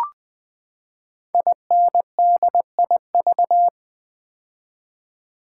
Morse code, heard as a keyed tone, about 20 wpm, 700 Hz. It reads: INDIV
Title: Build A Word - 5 Letters At A Time - 10-Letter Words - 20wpm